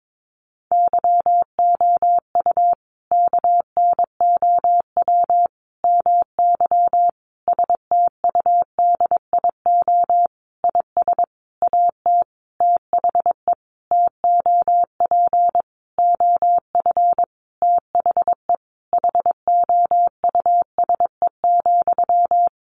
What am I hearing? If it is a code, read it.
Morse code, heard as a keyed tone, about 22 wpm, 700 Hz. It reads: YOU KNOW MY STUDIO IS AT THE TOP OF THE HOUSE,
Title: Sentences from Top 1200 Words 22wpm